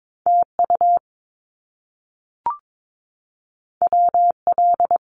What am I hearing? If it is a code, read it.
Morse code, heard as a keyed tone, about 22 wpm, 700 Hz. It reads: TU WL
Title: Abbreviations Repeated 3x - 22wpm